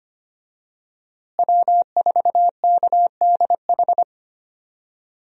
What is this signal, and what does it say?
Morse code, heard as a keyed tone, about 25 wpm, 700 Hz. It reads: W4KDH
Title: USA Call Signs - 5 Characters - Repeated 2x - 25wpm